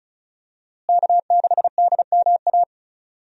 Morse code, heard as a keyed tone, about 36 wpm, 700 Hz. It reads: K6DMA
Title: USA Call Signs - 5 Characters - Repeated 2x - 35wpm